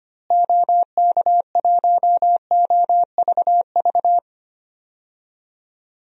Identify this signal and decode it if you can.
Morse code, heard as a keyed tone, about 25 wpm, 700 Hz. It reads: OK1OVV
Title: DXCC-100 Call Signs - 6 Characters - Repeated 2x - 25wpm